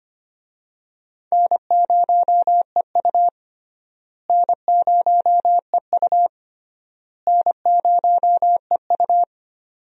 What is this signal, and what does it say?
Morse code, heard as a keyed tone, about 25 wpm, 700 Hz. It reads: N0EU N0EU N0EU
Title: USA Call Signs - 4 Characters - Repeated 3x - 25wpm